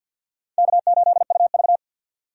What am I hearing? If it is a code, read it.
Morse code, heard as a keyed tone, about 47 wpm, 700 Hz. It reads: K8UV
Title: USA Call Signs - 4 Characters - 50wpm